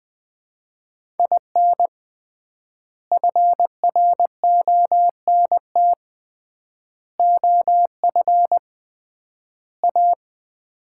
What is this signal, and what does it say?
Morse code, heard as a keyed tone, about 20 wpm, 700 Hz. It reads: IN FRONT OF A
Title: 3x Word-Spacing - Sets of 4 Words - 20wpm